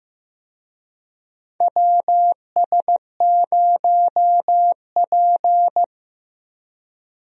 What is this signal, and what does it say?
Morse code, heard as a keyed tone, about 15 wpm, 700 Hz. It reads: WS0P
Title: USA Call Signs - 4 Characters - Repeated 3x - 15wpm